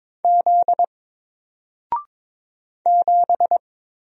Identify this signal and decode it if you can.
Morse code, heard as a keyed tone, about 22 wpm, 700 Hz. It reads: Z 7
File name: Single-Letter-Number-22wpm